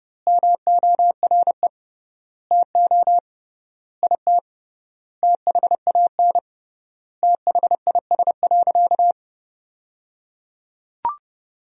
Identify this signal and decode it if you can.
Morse code, heard as a keyed tone, about 30 wpm, 700 Hz. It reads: MORE TO IT THAN THIS.
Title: 3x Word-Spacing - Sentences from Top 100 Words - 30wpm